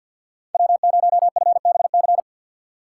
Morse code, heard as a keyed tone, about 53 wpm, 700 Hz. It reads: W0FBC
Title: USA Call Signs - 5 Characters - 4x Character Spacing - 50wpm